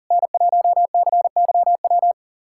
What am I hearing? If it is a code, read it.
Morse code, heard as a keyed tone, about 40 wpm, 690 Hz. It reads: N1CYW
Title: SOTA QSO Practice 40wpm